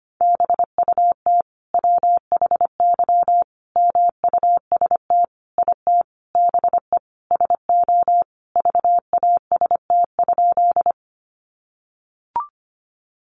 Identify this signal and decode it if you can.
Morse code, heard as a keyed tone, about 25 wpm, 700 Hz. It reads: BUT WHY MUST IT BE SO VAST?
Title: Sentences from Top 1300 Words 25wpm